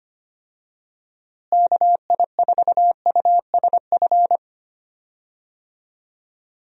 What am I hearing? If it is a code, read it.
Morse code, heard as a keyed tone, about 25 wpm, 700 Hz. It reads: KI4USF
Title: USA Call Signs - 6 Characters - Repeated 2x - 25wpm